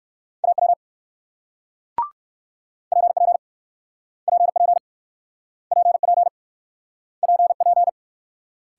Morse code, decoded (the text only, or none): AW PP PP PP PP